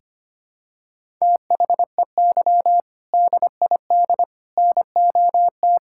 Morse code, heard as a keyed tone, about 25 wpm, 700 Hz. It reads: THEY DID NOT
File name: practice / 25wpm / Sentences with 4-Word Phrases - 25wpm